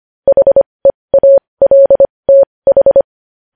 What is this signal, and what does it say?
Morse code, heard as a keyed tone, about 25 wpm, spaced as for 20 wpm, 550 Hz. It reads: HEALTH
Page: News Headlines in Morse Code at 20 WPM